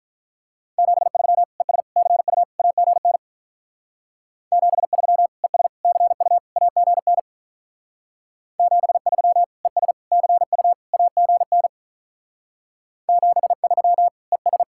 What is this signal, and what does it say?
Morse code, heard as a keyed tone, about 44 wpm, 700 Hz. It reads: 73 ES CU AGN 73 ES CU AGN 73 ES CU AGN 73 ES
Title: Speed-Racing Common QSO Phrases Encore - 35wpm